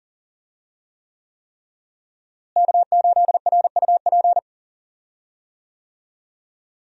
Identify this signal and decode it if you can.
Morse code, heard as a keyed tone, about 40 wpm, 700 Hz. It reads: K8RUP
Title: USA Call Signs - 5 Characters - Repeated 3x - 40wpm